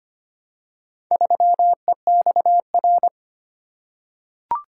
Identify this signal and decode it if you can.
Morse code, heard as a keyed tone, about 25 wpm, 700 Hz. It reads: 3EXR
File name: practice / 25wpm / Four Letter-Number Combinations - 25wpm